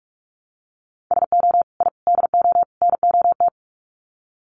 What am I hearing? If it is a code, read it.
Morse code, heard as a keyed tone, about 44 wpm, 700 Hz. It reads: SO I DO NOT
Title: Sets of 4 Words 45wpm